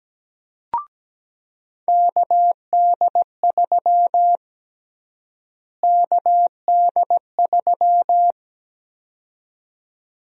Morse code, heard as a keyed tone, about 17 wpm, 700 Hz. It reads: KD3 KD3
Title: USA Call Sign Prefixes - Repeated 2x - 17wpm